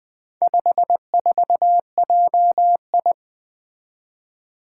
Morse code, heard as a keyed tone, about 20 wpm, 700 Hz. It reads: 54JI
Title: Four Letter-Number Combinations - 20wpm